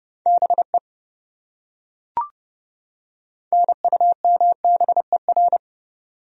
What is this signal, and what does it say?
Morse code, heard as a keyed tone, about 28 wpm, 700 Hz. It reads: BE NUMBER